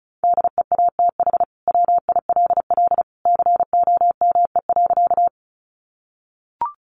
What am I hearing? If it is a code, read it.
Morse code, heard as a keyed tone, about 34 wpm, 700 Hz. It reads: DEATH WILL COME.
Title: Sentences from Top 1100 Words 35wpm